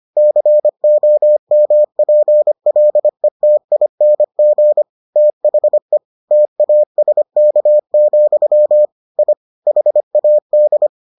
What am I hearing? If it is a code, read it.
Morse code, heard as a keyed tone, about 25 wpm, 600 Hz. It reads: COMPLETING THE TASK, I HAD